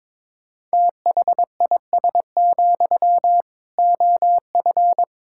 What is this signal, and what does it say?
Morse code, heard as a keyed tone, about 22 wpm, 700 Hz. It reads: THIS, OF
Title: Sentences from Top 1600 Words 22wpm